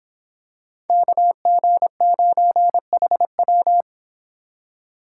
Morse code, heard as a keyed tone, about 26 wpm, 700 Hz. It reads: KG9HW